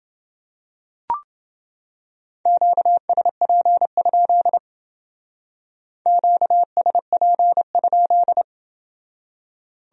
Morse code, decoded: QSP? QSP?

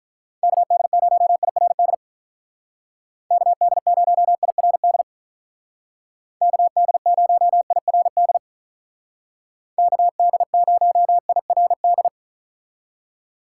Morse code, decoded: KD0IRD KD0IRD KD0IRD KD0IRD

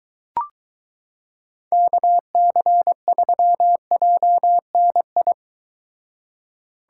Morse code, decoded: KC3JNI

